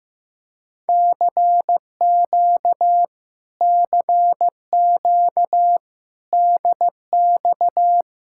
Morse code, heard as a keyed tone, about 15 wpm, 700 Hz. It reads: CQ CQ DX
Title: Speed-Racing Common QSO Phrases Encore - 15wpm